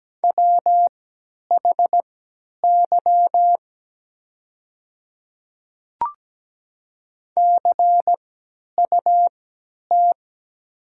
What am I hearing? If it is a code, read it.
Morse code, heard as a keyed tone, about 17 wpm, spaced as for 10 wpm, 700 Hz. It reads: WHY CUT
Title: 3x Character Spacing - Top 300 Words - 17wpm